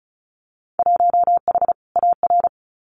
Morse code, heard as a keyed tone, about 36 wpm, 700 Hz. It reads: 1H AR
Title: Winter Field Day Exchanges 35wpm